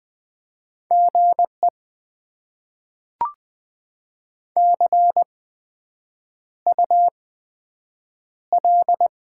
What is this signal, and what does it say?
Morse code, heard as a keyed tone, about 20 wpm, spaced as for 7 wpm, 700 Hz. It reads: GE CUL
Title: Top 90 QSO Elements - 8x Character Spacing - 20wpm